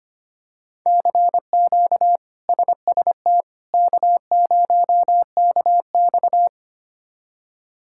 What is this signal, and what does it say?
Morse code, heard as a keyed tone, about 25 wpm, 700 Hz. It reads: CQ SST K0KX
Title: K1USN SST QSO Practice - 25wpm